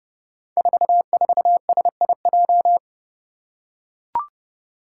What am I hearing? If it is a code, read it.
Morse code, heard as a keyed tone, about 30 wpm, 700 Hz. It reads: 44SIJ
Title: Five Challenging Letter-Number Combinations 30wpm